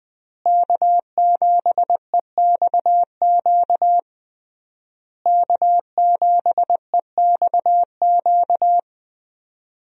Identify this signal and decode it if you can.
Morse code, heard as a keyed tone, about 21 wpm, 700 Hz. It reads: K7EXQ K7EXQ
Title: USA Call Signs - 5 Characters - Repeated 3x - 20wpm